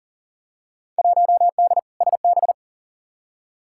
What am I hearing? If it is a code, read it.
Morse code, heard as a keyed tone, about 40 wpm, 700 Hz. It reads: 1D SB
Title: Field Day Exchanges - 40wpm